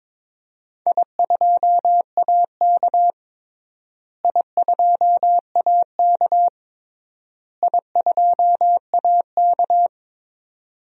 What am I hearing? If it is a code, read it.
Morse code, heard as a keyed tone, about 22 wpm, 700 Hz. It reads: I2AK I2AK I2AK
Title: DXCC-100 Call Signs - 4 Characters - Repeated 3x - 22wpm